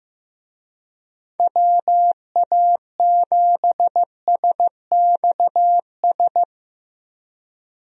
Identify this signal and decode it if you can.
Morse code, heard as a keyed tone, about 15 wpm, 700 Hz. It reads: WA7SXS
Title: USA Call Signs - 6 Characters - Repeated 3x - 15wpm